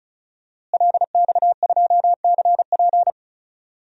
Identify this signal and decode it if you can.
Morse code, heard as a keyed tone, about 34 wpm, 700 Hz. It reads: LX2CP
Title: DXCC-100 Call Signs - 5 Characters - Repeated 3x - 35wpm